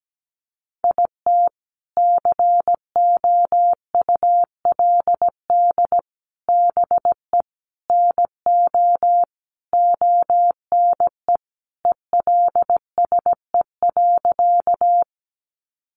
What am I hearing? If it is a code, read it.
Morse code, heard as a keyed tone, about 17 wpm, 700 Hz. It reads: IT COULD BE NO ONE ELSE.
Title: Sentences from Top 800 Words 17wpm